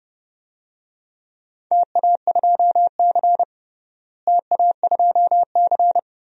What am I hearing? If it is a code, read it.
Morse code, heard as a keyed tone, about 30 wpm, 700 Hz. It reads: TA2C TA2C
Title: DXCC-100 Call Signs - 4 Characters - Repeated 3x - 30wpm